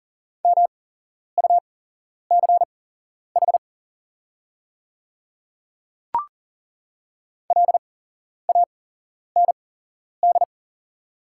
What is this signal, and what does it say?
Morse code, heard as a keyed tone, about 40 wpm, spaced as for 11 wpm, 690 Hz. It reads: MUCH LAND
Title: 8x Character Spacing - Top 300 Words - 40wpm